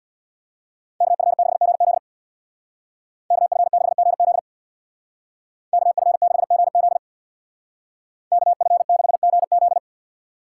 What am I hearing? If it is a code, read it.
Morse code, heard as a keyed tone, about 65 wpm, 690 Hz. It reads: KF6GZ KF6GZ KF6GZ KF6GZ